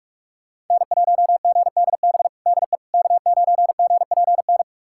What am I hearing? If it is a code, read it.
Morse code, heard as a keyed tone, about 45 wpm, 700 Hz. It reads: N1GDB DE K9GPN
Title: USA Callsign DE Callsign - 5 Characters - 45wpm